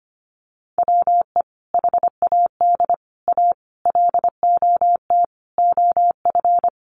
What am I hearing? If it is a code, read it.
Morse code, heard as a keyed tone, about 25 wpm, 700 Hz. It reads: WE HAD A LOT OF
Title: Sets of 5 Words Encore 25wpm